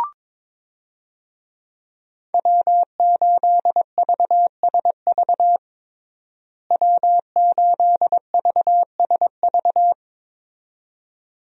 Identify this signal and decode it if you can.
Morse code, heard as a keyed tone, about 22 wpm, 700 Hz. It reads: W8VSV W8VSV
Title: USA Call Signs - 5 Characters - Repeated 2x - 22wpm